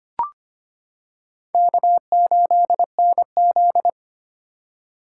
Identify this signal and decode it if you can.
Morse code, heard as a keyed tone, about 25 wpm, 700 Hz. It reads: K8NZ
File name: practice / 25wpm / Four Letter-Number Combinations - 25wpm